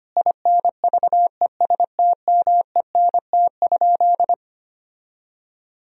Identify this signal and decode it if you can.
Morse code, heard as a keyed tone, about 25 wpm, 700 Hz. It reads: INVESTMENT?
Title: Sentences from Top 1000 Words - Review Part 4 - 25wpm